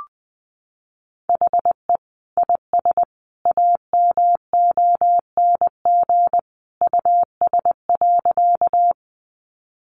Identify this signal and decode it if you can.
Morse code, heard as a keyed tone, about 20 wpm, 700 Hz. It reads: HE IS AMONG US.
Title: Sentences from Top 500 Words 20wpm